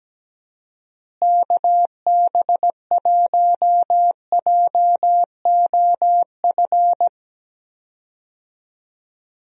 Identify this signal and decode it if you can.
Morse code, heard as a keyed tone, about 17 wpm, 700 Hz. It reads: KB1JOF
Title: USA Call Signs - 6 Characters - Repeated 2x - 17wpm